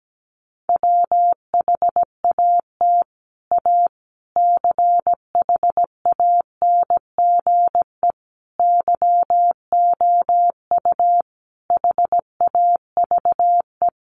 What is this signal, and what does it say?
Morse code, heard as a keyed tone, about 17 wpm, 700 Hz. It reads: WHAT A CHANGE YOU HAVE
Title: Sentences from Top 600 Words 17wpm